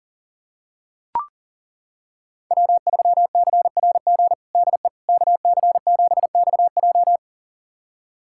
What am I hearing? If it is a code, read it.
Morse code, heard as a keyed tone, about 40 wpm, 700 Hz. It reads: W3CRG DE KC7XJ
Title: USA Callsign DE Callsign - 5 Characters - 40wpm